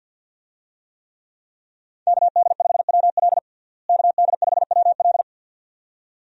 Morse code, heard as a keyed tone, about 50 wpm, 700 Hz. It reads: KD5WL KD5WL
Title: USA Call Signs - 5 Characters - Repeated 2x - 50wpm